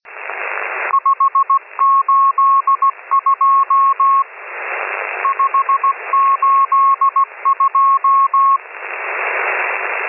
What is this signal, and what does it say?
Morse code, heard as a keyed tone, about 17 wpm, 1090 Hz, with radio noise. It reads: 582 582
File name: M12- -1010utc